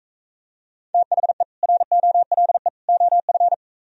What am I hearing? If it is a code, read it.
Morse code, heard as a keyed tone, about 43 wpm, 700 Hz. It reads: THE ROLE OF